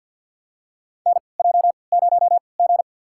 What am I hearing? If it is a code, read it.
Morse code, heard as a keyed tone, about 52 wpm, spaced as for 28 wpm, 700 Hz. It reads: NJ0G
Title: USA Call Signs - 4 Characters - 3x Character Spacing - 50wpm